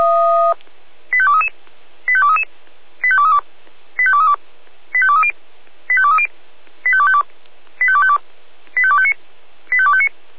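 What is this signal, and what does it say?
Five-tone selective call (ZVEI), heard as quick runs of five tones, tones of about 70 ms. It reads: 85233 85233 85232 85232 85230 85230 85252 85252 85259 85259
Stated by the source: ZVEI five-tone call 85852 received by SDR at feuerwehr-orlamuende (Orlamünde fire brigade).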